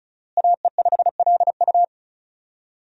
Morse code, heard as a keyed tone, about 36 wpm, 700 Hz. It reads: AE5LU
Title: USA Call Signs - 5 Characters - Repeated 3x - 35wpm